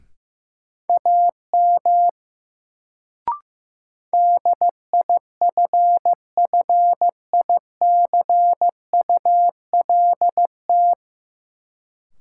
Morse code, heard as a keyed tone, about 15 wpm, 700 Hz. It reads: AM DIFFICULT
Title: Sentences from Top 600 Words - New Word Drill - 15wpm